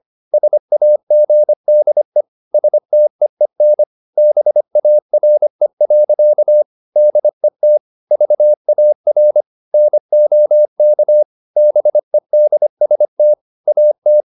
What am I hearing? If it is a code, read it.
Morse code, heard as a keyed tone, about 25 wpm, 600 Hz. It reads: SAGDE STEEN BARE. DET VAR NOK BEDST AT
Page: Ferie 25 wpm